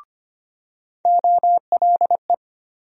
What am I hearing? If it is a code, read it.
Morse code, heard as a keyed tone, about 25 wpm, 700 Hz. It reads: OLE
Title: USA Call Sign Postfixes - 3 Letters 25wpm